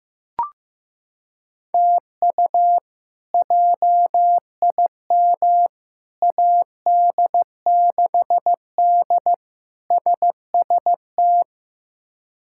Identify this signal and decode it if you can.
Morse code, heard as a keyed tone, about 15 wpm, 700 Hz. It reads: TU JIM AD6D SST
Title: K1USN SST QSO Practice - 15wpm